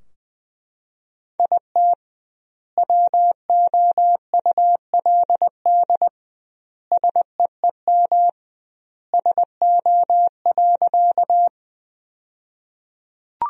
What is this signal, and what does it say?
Morse code, heard as a keyed tone, about 20 wpm, 700 Hz. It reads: IT WOULD SEEM SO.
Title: 2x Word-Spacing - Sentences from Top 300 Words - 20wpm